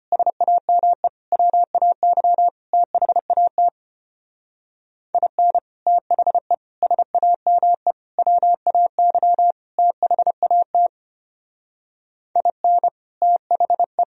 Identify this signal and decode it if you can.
Morse code, heard as a keyed tone, about 31 wpm, 700 Hz. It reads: SAME WAY THAT IN THE SAME WAY THAT IN THE